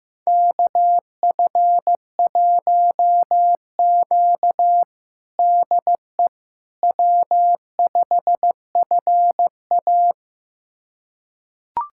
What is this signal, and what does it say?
Morse code, heard as a keyed tone, about 15 wpm, 700 Hz. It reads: KF1Q DE W5FA
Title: USA Callsign DE Callsign - 4 Characters - 15wpm